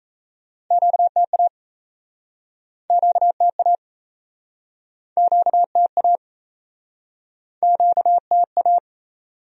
Morse code, heard as a keyed tone, about 40 wpm, 700 Hz. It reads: QTA QTA QTA QTA